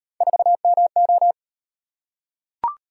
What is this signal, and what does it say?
Morse code, heard as a keyed tone, about 38 wpm, 700 Hz. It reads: 4MO